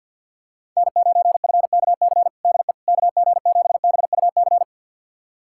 Morse code, heard as a keyed tone, about 50 wpm, 700 Hz. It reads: N9FKC DE KG7BUC
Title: USA Callsign DE Callsign 50wpm